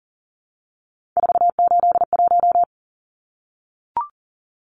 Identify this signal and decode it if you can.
Morse code, heard as a keyed tone, about 40 wpm, 700 Hz. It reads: 481
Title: Numbers - 3-Digit 40wpm